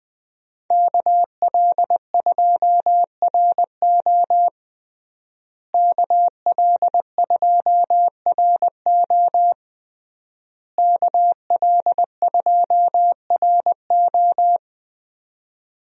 Morse code, heard as a keyed tone, about 21 wpm, 700 Hz. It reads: KL2RO KL2RO KL2RO
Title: USA Call Signs - 5 Characters - Repeated 3x - 20wpm